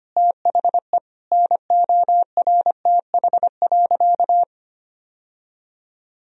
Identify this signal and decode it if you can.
Morse code, heard as a keyed tone, about 25 wpm, 700 Hz. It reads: THE NORTH.
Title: Sentences from Top 1000 Words - Review Part 1 - 25wpm